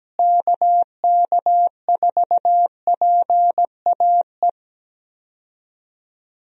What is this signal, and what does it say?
Morse code, heard as a keyed tone, about 17 wpm, 700 Hz. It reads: KK4PAE
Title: USA Call Signs - 6 Characters - Repeated 3x - 17wpm